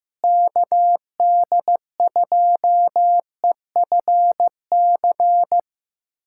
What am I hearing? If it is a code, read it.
Morse code, heard as a keyed tone, about 15 wpm, 700 Hz. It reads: KD2EFC
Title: USA Callsign DE Callsign 15wpm